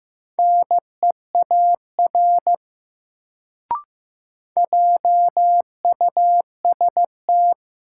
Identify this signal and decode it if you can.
Morse code, heard as a keyed tone, about 16 wpm, 700 Hz. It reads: NEAR JUST
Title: Sentences from Top 200 Words - New Word Drill - 15wpm